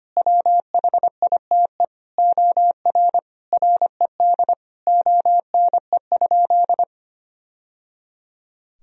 Morse code, heard as a keyed tone, about 25 wpm, 700 Hz. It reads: WHITE OR RED ONE?
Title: Sentences from Top 1000 Words - Review Part 2 - 25wpm